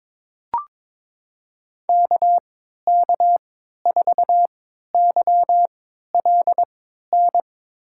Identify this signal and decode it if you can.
Morse code, heard as a keyed tone, about 22 wpm, spaced as for 13 wpm, 700 Hz. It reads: KK4YLN
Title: USA Call Signs - 6 Characters - 3x Character Spacing - 22wpm